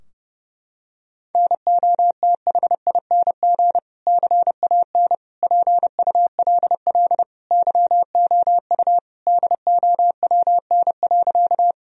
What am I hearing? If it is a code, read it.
Morse code, heard as a keyed tone, about 30 wpm, 700 Hz. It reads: NOTHING CAN PULL YOU DOWN.
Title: Sentences from Top 1000 Words - Review Part 2 - 30wpm